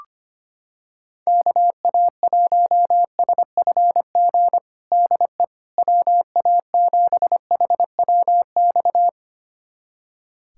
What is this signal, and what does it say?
Morse code, heard as a keyed tone, about 25 wpm, 700 Hz. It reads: KA1SFG DE WA7HWX
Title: USA Callsign DE Callsign 25wpm